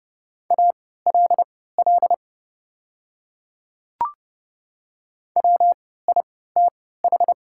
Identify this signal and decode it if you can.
Morse code, heard as a keyed tone, about 30 wpm, spaced as for 17 wpm, 700 Hz. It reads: ALL WITH
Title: 3x Character Spacing - Top 100 Words - 30wpm